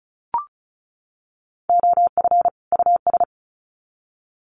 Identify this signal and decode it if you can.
Morse code, heard as a keyed tone, about 34 wpm, 700 Hz. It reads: OF US
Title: Sets of 2 Words 35wpm